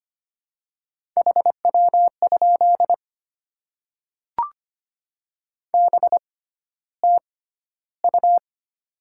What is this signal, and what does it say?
Morse code, heard as a keyed tone, about 25 wpm, spaced as for 13 wpm, 700 Hz. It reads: HW? BTU
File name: Top 20 QSO Elements - 6x Character Spacing - 25wpm